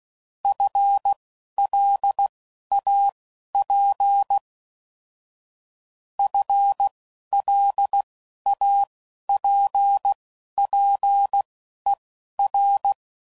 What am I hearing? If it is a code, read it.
Morse code, heard as a keyed tone, about 16 wpm, spaced as for 11 wpm, 800 Hz. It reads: FLAP FLAPPER